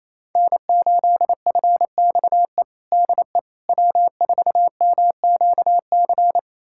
Morse code, heard as a keyed tone, about 28 wpm, 700 Hz. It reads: N8FXE DE W4MQC
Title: USA Callsign DE Callsign 28wpm